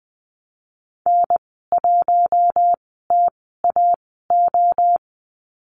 Morse code, heard as a keyed tone, about 20 wpm, spaced as for 14 wpm, 700 Hz. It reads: N1TAO
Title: 042 Lesson 8g - Number 1 - Call Signs 20wpm